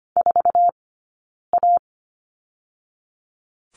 Morse code, heard as a keyed tone, about 25 wpm, 700 Hz. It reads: 4 A
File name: course / 25:10wpm / 099 Lesson 17d - Letter u - Two Characters 25wpm